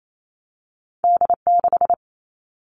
Morse code, heard as a keyed tone, about 28 wpm, 700 Hz. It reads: D6
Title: All Cut Numbers - 2-Digit 28wpm